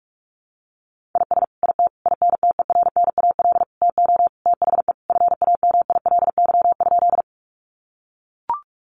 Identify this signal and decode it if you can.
Morse code, heard as a keyed tone, about 44 wpm, 700 Hz. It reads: IS IT INTERNAL TO THE FAMILY?
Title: Sentences from Top 1400 Words 45wpm